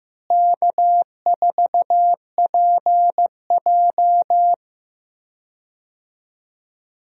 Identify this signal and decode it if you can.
Morse code, heard as a keyed tone, about 15 wpm, 700 Hz. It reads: K4PJ
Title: USA Call Signs - 4 Characters - Repeated 3x - 15wpm